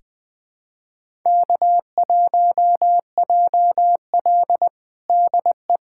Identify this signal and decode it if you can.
Morse code, heard as a keyed tone, about 21 wpm, 700 Hz. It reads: K1JL DE
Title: USA Callsign DE Callsign - 4 Characters - 20wpm